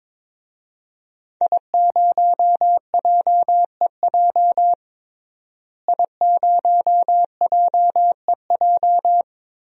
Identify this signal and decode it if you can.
Morse code, heard as a keyed tone, about 22 wpm, 700 Hz. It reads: I0JEJ I0JEJ
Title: DXCC-100 Call Signs - 5 Characters - Repeated 2x - 22wpm